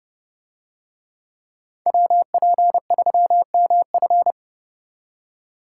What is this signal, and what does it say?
Morse code, heard as a keyed tone, about 29 wpm, 700 Hz. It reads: WP3MF